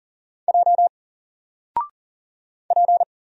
Morse code, heard as a keyed tone, about 40 wpm, 700 Hz. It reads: J P
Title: Single Character (Letter:Number:Punctuation:Prosign) Alt 40wpm